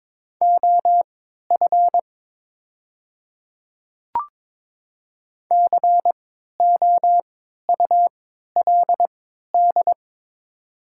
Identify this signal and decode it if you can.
Morse code, heard as a keyed tone, about 22 wpm, spaced as for 12 wpm, 700 Hz. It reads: OF COULD
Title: 3x Character Spacing - Top 100 Words - 22wpm